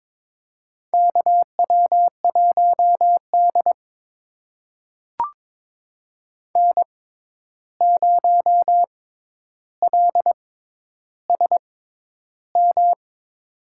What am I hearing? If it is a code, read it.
Morse code, heard as a keyed tone, about 22 wpm, spaced as for 11 wpm, 700 Hz. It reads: KW1D N0LSM
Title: USA Call Signs - 6x Character Spacing - 22wpm